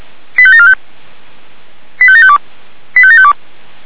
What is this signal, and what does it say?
Five-tone selective call (ZVEI), heard as quick runs of five tones, tones of about 70 ms. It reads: 85745 85752 85752